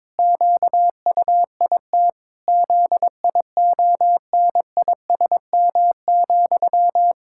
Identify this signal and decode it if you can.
Morse code, heard as a keyed tone, about 22 wpm, 700 Hz. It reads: QUIT ZIONISM,
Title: Pangram Challenge v2 - 22wpm